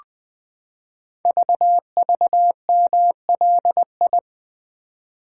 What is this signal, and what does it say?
Morse code, heard as a keyed tone, about 21 wpm, 700 Hz. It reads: VVMLI
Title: Code Groups - 5 Letters - 20wpm